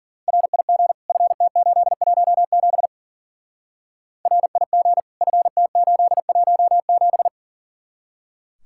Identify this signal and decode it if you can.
Morse code, heard as a keyed tone, about 49 wpm, 700 Hz. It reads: RIG FT817 RIG FT817
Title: Speed-Racing Common QSO Phrases - 40wpm